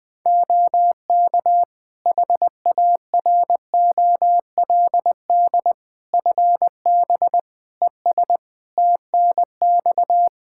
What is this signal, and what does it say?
Morse code, heard as a keyed tone, about 21 wpm, 700 Hz. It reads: OK HAROLD FB ES TNX
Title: Practice QSO - Exchange 2 - Info - 20wpm